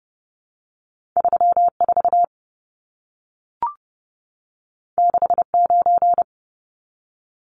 Morse code, heard as a keyed tone, about 31 wpm, 700 Hz. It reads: 34 69
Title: Numbers - 2-Digit 30wpm